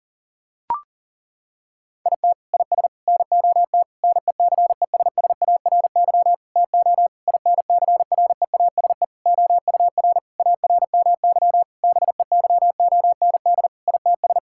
Morse code, heard as a keyed tone, about 40 wpm, 700 Hz. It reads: IT IS NOT NECESSARY TO INCREASE OUR ARMY BEYOND ITS
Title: Sentences from Top 1000 Words - Review Part 6 - 40wpm